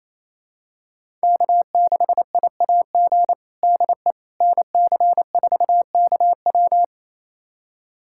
Morse code, heard as a keyed tone, about 28 wpm, 700 Hz. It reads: K6IAG DE NC4KW
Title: USA Callsign DE Callsign - 5 Characters - 28wpm